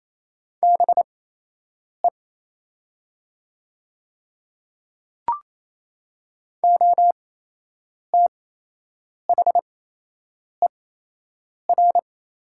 Morse code, heard as a keyed tone, about 28 wpm, spaced as for 8 wpm, 700 Hz. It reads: BE OTHER
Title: 8x Character Spacing - Top 100 Words - 28wpm